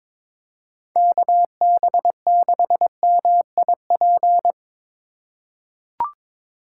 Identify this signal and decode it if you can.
Morse code, heard as a keyed tone, about 22 wpm, 700 Hz. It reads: KB6MIP
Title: USA Call Signs - 6 Characters - 2x Character Spacing - 22wpm